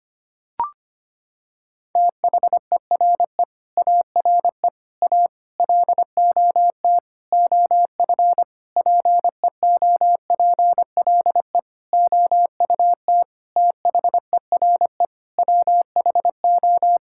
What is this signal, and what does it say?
Morse code, heard as a keyed tone, about 25 wpm, 700 Hz. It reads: THERE ARE A LOT OF PEOPLE OUT THERE WHO
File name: Sets of 9 Words - 25wpm